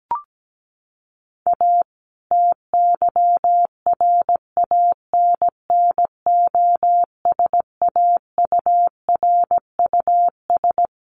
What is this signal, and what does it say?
Morse code, heard as a keyed tone, about 17 wpm, 700 Hz. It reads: A TYRANNOSAURUS
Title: A Sound Of Thunder 17wpm